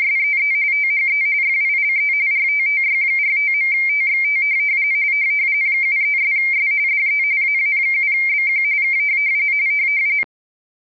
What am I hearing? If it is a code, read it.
RTTY, about 45.45 baud. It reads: SEE THE FILE RTTY45.TXT 73'S DE AA7BQ K